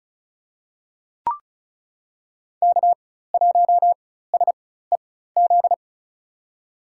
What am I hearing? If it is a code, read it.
Morse code, heard as a keyed tone, about 36 wpm, spaced as for 16 wpm, 700 Hz. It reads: K1SEZ
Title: USA Call Signs - 5 Characters - 4x Character Spacing - 35wpm